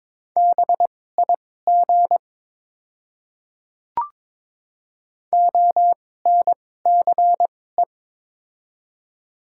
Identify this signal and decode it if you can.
Morse code, heard as a keyed tone, about 22 wpm, spaced as for 16 wpm, 700 Hz. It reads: BIG ONCE
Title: 2x Character Spacing - Top 300 Words - 22wpm